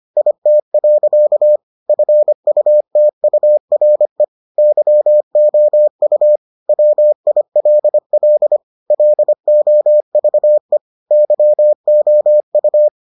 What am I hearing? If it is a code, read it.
Morse code, heard as a keyed tone, about 25 wpm, 600 Hz. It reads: IT. FUTURE YOU WILL LOVE YOU